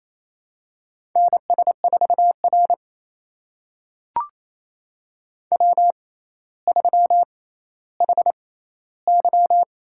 Morse code, heard as a keyed tone, about 28 wpm, spaced as for 14 wpm, 700 Hz. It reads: NS4R W3HY
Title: USA Call Signs - 4 Characters - 6x Character Spacing - 28wpm